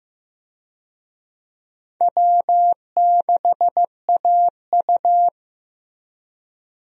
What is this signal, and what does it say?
Morse code, heard as a keyed tone, about 16 wpm, 700 Hz. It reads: W6AU
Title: USA Call Signs - 4 Characters - Repeated 2x - 15wpm